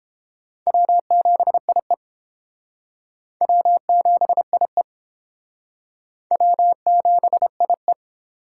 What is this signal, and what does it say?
Morse code, heard as a keyed tone, about 31 wpm, 700 Hz. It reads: W7IE W7IE W7IE